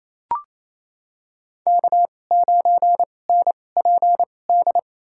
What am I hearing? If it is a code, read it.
Morse code, heard as a keyed tone, about 28 wpm, spaced as for 20 wpm, 700 Hz. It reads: K9NPD
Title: USA Call Signs - 2x Character Spacing - 28wpm